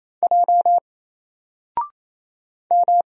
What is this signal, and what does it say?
Morse code, heard as a keyed tone, about 28 wpm, 700 Hz. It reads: J M
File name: Single-Letter-Number-28wpm